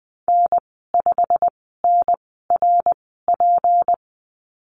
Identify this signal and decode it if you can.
Morse code, heard as a keyed tone, about 20 wpm, spaced as for 14 wpm, 700 Hz. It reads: N5NRP
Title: 161 Lesson 26g - Letter g - Call Signs 20wpm